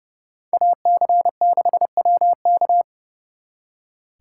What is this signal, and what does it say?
Morse code, heard as a keyed tone, about 37 wpm, 700 Hz. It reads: AC6WK
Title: Speed Racing - USA 2x2 Call Signs 30wpm